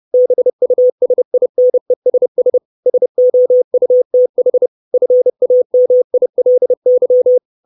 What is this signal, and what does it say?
Morse code, heard as a keyed tone, about 30 wpm, 500 Hz. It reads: BUSINESS SOUTH FAMILY
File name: 632 Words 30WPM